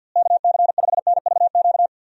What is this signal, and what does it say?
Morse code, heard as a keyed tone, about 51 wpm, 700 Hz. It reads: KC5NVQ
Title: USA Call Signs - 6 Characters - Repeated 2x - 50wpm